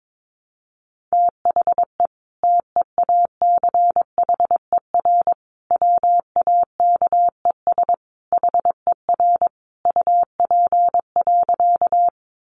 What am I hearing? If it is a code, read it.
Morse code, heard as a keyed tone, about 22 wpm, 700 Hz. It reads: THE TEACHER WAKES HER UP.
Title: Easy Phrases 22wpm